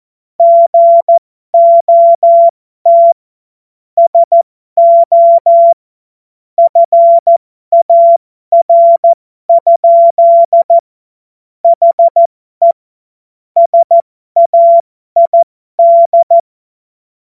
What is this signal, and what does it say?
Morse code, heard as a keyed tone, about 14 wpm, spaced as for 12 wpm, 680 Hz. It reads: GOT SO FAR? HE SAID